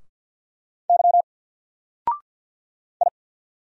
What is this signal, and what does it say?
Morse code, heard as a keyed tone, about 48 wpm, 700 Hz. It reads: Y I
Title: Single Character (Letter:Number:Punctuation:Prosign) 50wpm